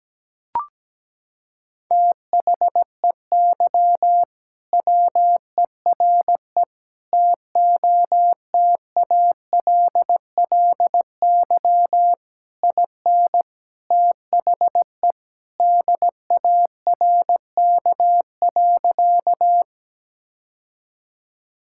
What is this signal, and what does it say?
Morse code, heard as a keyed tone, about 17 wpm, 700 Hz. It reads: THEY WERE TOTALLY IN THE DARK.
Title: Sentences from Top 1600 Words 17wpm